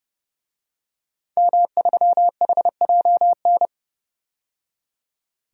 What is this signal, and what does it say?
Morse code, heard as a keyed tone, about 32 wpm, 700 Hz. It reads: M3HJN
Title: DXCC-100 Call Signs - 5 Characters - 30wpm